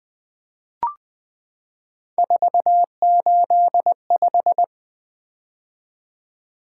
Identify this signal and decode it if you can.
Morse code, heard as a keyed tone, about 20 wpm, 700 Hz. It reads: 485
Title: Three Letter-Number Combinations - 20wpm